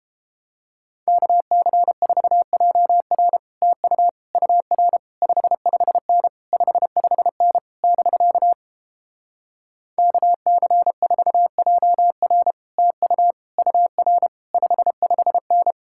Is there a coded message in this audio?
Yes: Morse code from 1.1 to 15.7 s, heard as a keyed tone, about 29 wpm, 700 Hz. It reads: KC4JR TU UR 55N 55N <BK> KC4JR TU UR 55N